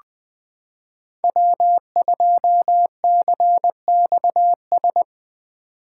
Morse code, heard as a keyed tone, about 20 wpm, 700 Hz. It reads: W2CXS
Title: Code Groups - 5 Letter:Numbers - 20wpm